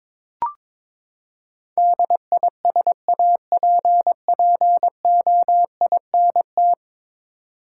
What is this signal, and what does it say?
Morse code, heard as a keyed tone, about 22 wpm, 700 Hz. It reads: DISAPPOINT
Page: morse-code-ninja-practice-22wpm